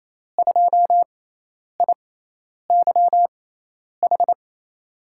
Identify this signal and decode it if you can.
Morse code, heard as a keyed tone, about 28 wpm, spaced as for 10 wpm, 700 Hz. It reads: 2IYH
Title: USA Call Signs - 5 Characters - 6x Character Spacing - 28wpm